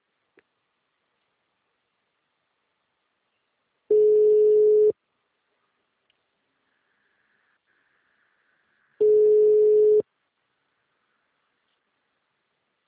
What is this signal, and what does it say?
Ringback tone (Europe, 425 Hz).